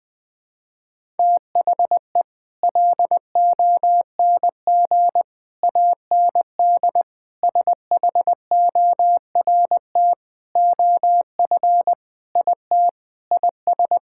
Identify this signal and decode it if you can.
Morse code, heard as a keyed tone, about 20 wpm, 700 Hz. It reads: THE LONG AND SHORT OF IT IS
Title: Sets of 8 Words - 20wpm